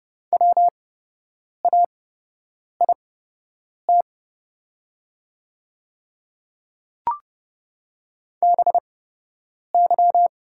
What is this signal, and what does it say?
Morse code, heard as a keyed tone, about 30 wpm, spaced as for 8 wpm, 700 Hz. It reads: WAIT BY
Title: 8x Character Spacing - Top 500 Words - 30wpm